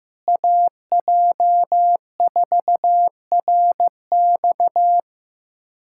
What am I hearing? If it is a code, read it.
Morse code, heard as a keyed tone, about 15 wpm, 700 Hz. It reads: AJ4RX
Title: USA Callsign DE Callsign - 5 Characters - 15wpm